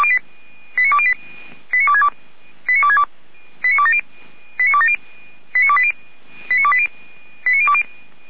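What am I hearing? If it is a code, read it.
Five-tone selective call (ZVEI), heard as quick runs of five tones, tones of about 70 ms. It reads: 89208 89262 89262 89270 89270 89280 89280 89020
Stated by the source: ZVEI five-tone selective call received by SDR.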